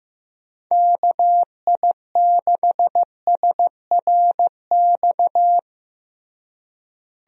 Morse code, heard as a keyed tone, about 16 wpm, 700 Hz. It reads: KI6SRX
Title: USA Call Signs - 6 Characters - Repeated 2x - 15wpm